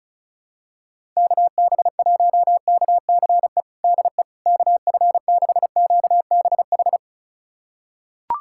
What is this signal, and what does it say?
Morse code, heard as a keyed tone, about 36 wpm, 700 Hz. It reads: KB1KCE DE KF6QBH
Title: USA Callsign DE Callsign 35wpm